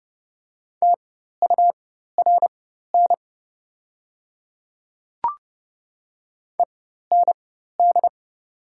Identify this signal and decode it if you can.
Morse code, heard as a keyed tone, about 29 wpm, spaced as for 14 wpm, 700 Hz. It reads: TURN END
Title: 4x Character Spacing - Top 200 Words - 30wpm